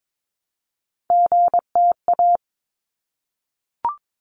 Morse code, heard as a keyed tone, about 22 wpm, 700 Hz. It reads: GTA
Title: ARRL Sections 22wpm